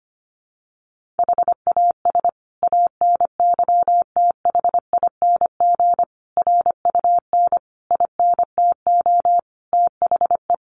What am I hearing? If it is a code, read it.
Morse code, heard as a keyed tone, about 25 wpm, 700 Hz. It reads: HAS ANYTHING RUN INTO THE
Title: Sentences from Top 1400 Words 25wpm